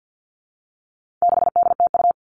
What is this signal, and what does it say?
Morse code, heard as a keyed tone, about 50 wpm, 700 Hz. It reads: 6DTU
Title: All Cut Numbers - 4-Digit 50wpm